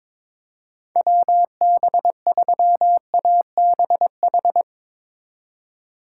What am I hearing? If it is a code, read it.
Morse code, heard as a keyed tone, about 22 wpm, 700 Hz. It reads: WB3ABH
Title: USA Call Signs - 4x Character Spacing - 22wpm